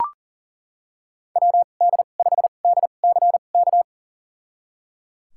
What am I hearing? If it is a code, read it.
Morse code, heard as a keyed tone, about 37 wpm, spaced as for 28 wpm, 700 Hz. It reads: WD5DCK
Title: USA Call Signs - 6 Characters - 2x Character Spacing - 40wpm